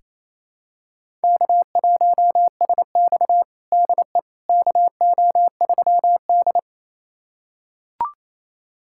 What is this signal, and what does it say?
Morse code, heard as a keyed tone, about 28 wpm, 700 Hz. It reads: K1SX DE KO3D
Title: USA Callsign DE Callsign - 4 Characters - 28wpm